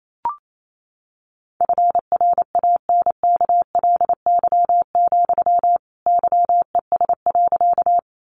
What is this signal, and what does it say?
Morse code, heard as a keyed tone, about 28 wpm, 700 Hz. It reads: FRANKLY, YES.
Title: A Sound Of Thunder 28wpm